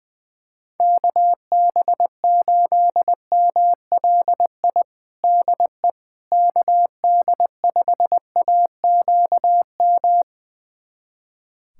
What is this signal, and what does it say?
Morse code, heard as a keyed tone, about 21 wpm, 700 Hz. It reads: KB8MLI DE KD5AQM
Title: USA Callsign DE Callsign 20wpm